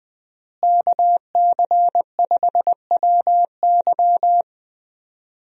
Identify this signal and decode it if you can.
Morse code, heard as a keyed tone, about 20 wpm, 700 Hz. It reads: KC5WY
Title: USA Call Signs - 5 Characters - Repeated 3x - 20wpm